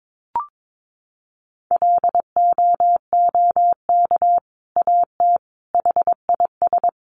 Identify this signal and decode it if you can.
Morse code, heard as a keyed tone, about 22 wpm, 700 Hz. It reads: LOOK AT HIS
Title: Sentences from Top 1500 Words 22wpm